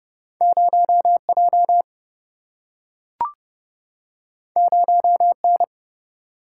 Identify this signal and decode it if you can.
Morse code, heard as a keyed tone, about 30 wpm, 700 Hz. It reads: 0J 0N
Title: Single-Letter Single-Number 30wpm